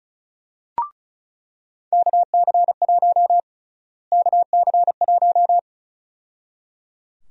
Morse code, heard as a keyed tone, about 34 wpm, 700 Hz. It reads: KC1 KC1
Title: USA Call Sign Prefixes - Repeated 2x - 35wpm